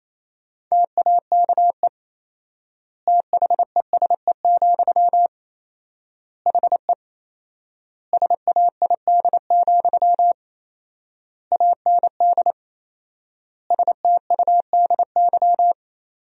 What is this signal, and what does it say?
Morse code, heard as a keyed tone, about 28 wpm, 700 Hz. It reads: TAKE THESE, HE SAID, AND STUDY
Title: 4x Word-Spacing - Sentences from Top 300 Words - 28wpm